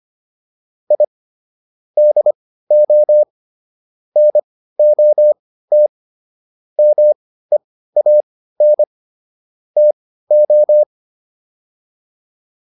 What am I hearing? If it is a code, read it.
Morse code, heard as a keyed tone, about 25 wpm, spaced as for 15 wpm, 600 Hz. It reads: I DO NOT MEAN TO